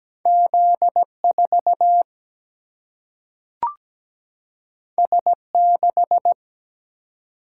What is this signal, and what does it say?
Morse code, heard as a keyed tone, about 17 wpm, 700 Hz. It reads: Z4 S6
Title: Single-Letter Single-Number 17wpm